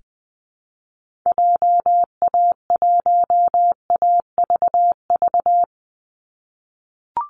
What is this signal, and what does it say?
Morse code, heard as a keyed tone, about 20 wpm, 700 Hz. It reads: JA1AVV
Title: USA Call Signs 20wpm